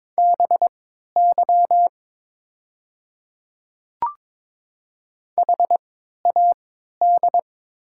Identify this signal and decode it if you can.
Morse code, heard as a keyed tone, about 23 wpm, spaced as for 13 wpm, 700 Hz. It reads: BY HAD
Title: 3x Character Spacing - Top 100 Words - 22wpm